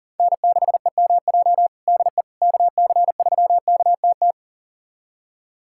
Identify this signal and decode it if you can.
Morse code, heard as a keyed tone, about 40 wpm, 700 Hz. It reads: N6EMJ DE KC3KTT
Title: USA Callsign DE Callsign 40wpm